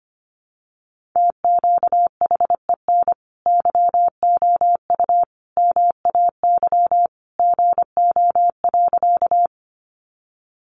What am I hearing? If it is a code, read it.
Morse code, heard as a keyed tone, about 25 wpm, 700 Hz. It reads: TQHEN YOU MAY GO.
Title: Sentences from Top 100 Words 25wpm